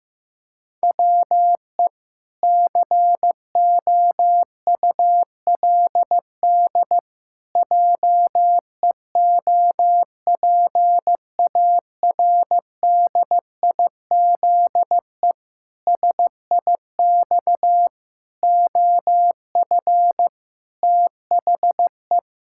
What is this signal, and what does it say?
Morse code, heard as a keyed tone, about 15 wpm, 700 Hz. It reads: WE COULD JEOPARDIZE SIX OF THE
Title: Pangram Challenge v2 - 15wpm